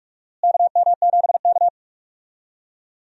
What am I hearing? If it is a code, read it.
Morse code, heard as a keyed tone, about 46 wpm, 700 Hz. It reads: KM7K
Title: USA Call Signs - 4 Characters - Repeated 3x - 45wpm